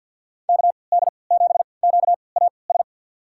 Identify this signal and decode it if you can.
Morse code, heard as a keyed tone, about 46 wpm, spaced as for 27 wpm, 690 Hz. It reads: KD7QAS
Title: USA Call Signs - 6 Characters - 3x Character Spacing - 50wpm